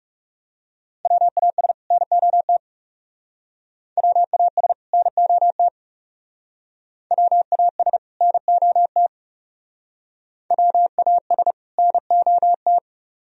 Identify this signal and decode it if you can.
Morse code, heard as a keyed tone, about 37 wpm, 700 Hz. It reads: WAS NOT WAS NOT WAS NOT WAS NOT
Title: Speed-Racing - Sets of 2 Words 30wpm